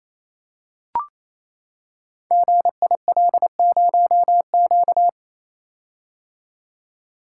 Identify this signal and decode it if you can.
Morse code, heard as a keyed tone, about 28 wpm, 700 Hz. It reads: GIL0Q
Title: Code Groups - 5 Letter:Numbers - 28wpm